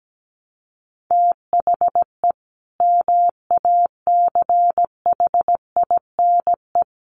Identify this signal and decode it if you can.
Morse code, heard as a keyed tone, about 17 wpm, 700 Hz. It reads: THE MACHINE
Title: A Sound Of Thunder 17wpm